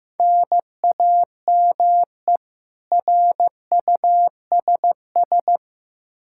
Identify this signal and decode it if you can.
Morse code, heard as a keyed tone, about 15 wpm, 700 Hz. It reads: NAME RUSS
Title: Practice QSO - Exchange 1 - Beginning - 15wpm